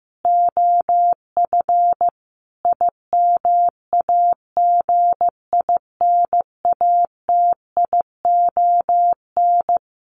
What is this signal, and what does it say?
Morse code, heard as a keyed tone, about 15 wpm, 700 Hz. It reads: OF IMAGINATION
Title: Sentences from Top 1300 Words 15wpm